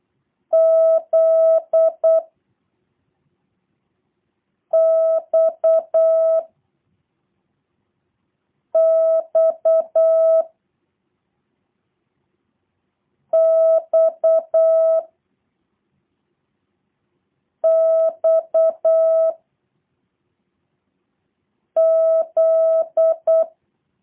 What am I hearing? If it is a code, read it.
Morse code, heard as a keyed tone, about 8 wpm, 650 Hz. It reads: Z X X X X Z